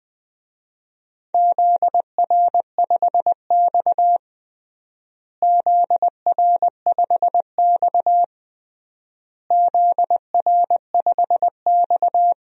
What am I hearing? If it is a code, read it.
Morse code, heard as a keyed tone, about 20 wpm, 700 Hz. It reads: ZR5X ZR5X ZR5X
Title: DXCC-100 Call Signs - 4 Characters - Repeated 3x - 20wpm